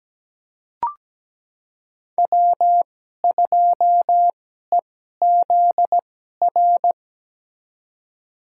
Morse code, heard as a keyed tone, about 17 wpm, spaced as for 12 wpm, 700 Hz. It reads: W2EZR
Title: USA Call Signs - 5 Characters - 2x Character Spacing - 17wpm